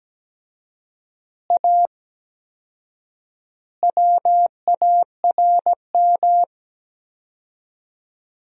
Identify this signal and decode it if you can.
Morse code, heard as a keyed tone, about 17 wpm, 700 Hz. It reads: A WARM
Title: 4x Word-Spacing - Sentences from Top 500 Words - 17wpm